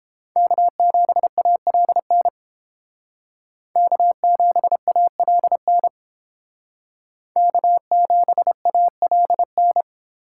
Morse code, heard as a keyed tone, about 28 wpm, 700 Hz. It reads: K7ALN K7ALN K7ALN